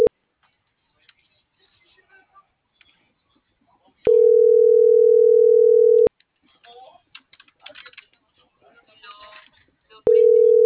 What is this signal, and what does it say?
Ringback tone (North America), 440 + 480 Hz.
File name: CENTRAL MULTIMODAL / out josimar.cruz - .76109